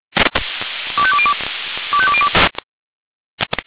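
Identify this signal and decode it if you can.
Five-tone selective call (ZVEI), heard as quick runs of five tones, tones of about 70 ms. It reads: 25202 25202
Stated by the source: ZVEI five-tone selective call received on an SDR.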